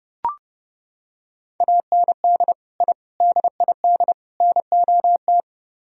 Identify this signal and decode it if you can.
Morse code, heard as a keyed tone, about 30 wpm, 700 Hz. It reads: AND I DID NOT